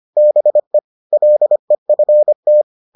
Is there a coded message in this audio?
Yes: Morse code from 0.2 to 2.6 s, heard as a keyed tone, about 25 wpm, 600 Hz. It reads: BE LEFT